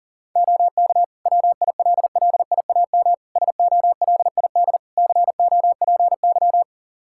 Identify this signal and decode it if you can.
Morse code, heard as a keyed tone, about 38 wpm, 700 Hz. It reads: OK WILLIAM SOLID COPY
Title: Practice QSO - Exchange 3 - More Info - 40wpm